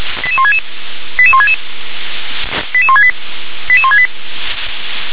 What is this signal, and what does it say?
Five-tone selective call (ZVEI), heard as quick runs of five tones, tones of about 70 ms. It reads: 88166 88166 88168 88168